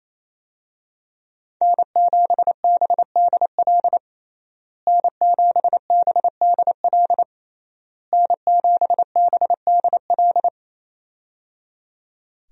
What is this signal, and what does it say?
Morse code, heard as a keyed tone, about 28 wpm, 700 Hz. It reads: N7BDL N7BDL N7BDL
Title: USA Call Signs - 5 Characters - Repeated 3x - 28wpm